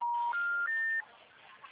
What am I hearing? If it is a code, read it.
Special information tone.
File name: out- -2015- - - .86156